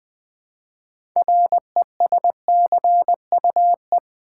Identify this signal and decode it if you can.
Morse code, heard as a keyed tone, about 20 wpm, 700 Hz. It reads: RESCUE
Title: 6 Letter Words - First 3 Letters Given - 20wpm